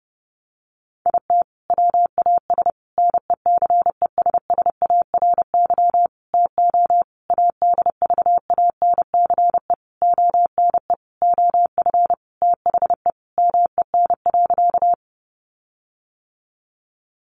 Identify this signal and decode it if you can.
Morse code, heard as a keyed tone, about 30 wpm, 700 Hz. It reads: IT WAS NECESSARY TO ADVANCE ONE OF THE MEN.
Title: Sentences from Top 1400 Words 30wpm